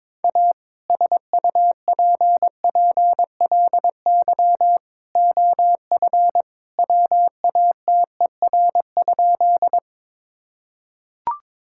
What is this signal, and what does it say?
Morse code, heard as a keyed tone, about 22 wpm, 700 Hz. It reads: A SUPPLY OF WATER?
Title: Sentences from Top 1000 Words - Review Part 5 - 22wpm